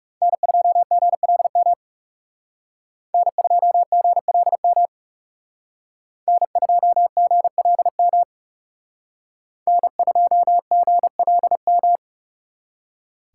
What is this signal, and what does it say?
Morse code, heard as a keyed tone, about 38 wpm, 700 Hz. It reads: N2GLM N2GLM N2GLM N2GLM